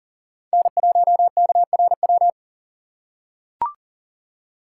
Morse code, heard as a keyed tone, about 40 wpm, 700 Hz. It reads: N1KRW
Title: USA Call Signs - 5 Characters - 4x Character Spacing - 40wpm